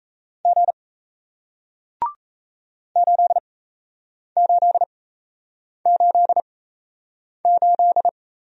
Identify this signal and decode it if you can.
Morse code, heard as a keyed tone, about 38 wpm, spaced as for 8 wpm, 700 Hz. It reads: G 8888